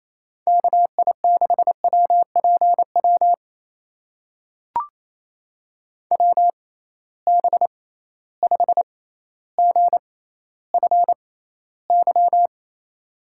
Morse code, heard as a keyed tone, about 28 wpm, spaced as for 14 wpm, 700 Hz. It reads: KI6WPW WB5GFY
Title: USA Call Signs - 6 Characters - 6x Character Spacing - 28wpm